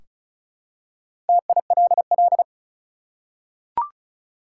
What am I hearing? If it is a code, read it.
Morse code, heard as a keyed tone, about 34 wpm, 700 Hz. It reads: TILL
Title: Four Letter Words - 35wpm